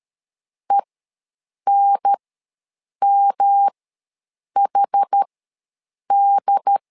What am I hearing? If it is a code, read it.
Morse code, heard as a keyed tone, about 13 wpm, spaced as for 7 wpm, 800 Hz. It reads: ENMHD
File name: LEZIONI / 3 MWRO / 3 M ws 12wpm 1745